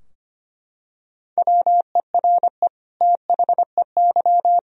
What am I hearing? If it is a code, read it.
Morse code, heard as a keyed tone, about 25 wpm, 700 Hz. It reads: WERE THEY
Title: Sentences from Top 1000 Words - Review Part 3 - 25wpm